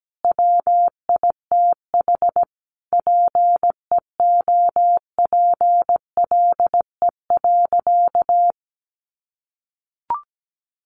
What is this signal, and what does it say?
Morse code, heard as a keyed tone, about 17 wpm, 700 Hz. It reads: WITH PEOPLE.
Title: Sentences from Top 1100 Words 17wpm